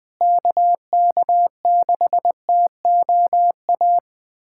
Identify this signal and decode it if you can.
Morse code, heard as a keyed tone, about 21 wpm, 700 Hz. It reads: KK6TOA
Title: USA Callsign DE Callsign - 6 Characters - 20wpm